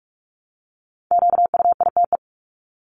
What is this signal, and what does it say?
Morse code, heard as a keyed tone, about 44 wpm, 700 Hz. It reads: QUITE
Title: Top 501-750 Words 45wpm